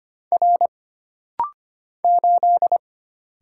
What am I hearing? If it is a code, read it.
Morse code, heard as a keyed tone, about 25 wpm, 700 Hz. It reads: R 8
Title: Single Letter-Number - Rapid-Fire 25wpm